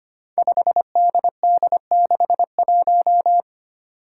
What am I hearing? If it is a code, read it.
Morse code, heard as a keyed tone, about 25 wpm, 700 Hz. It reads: 5DD61
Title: Five Challenging Letter-Number Combinations 25wpm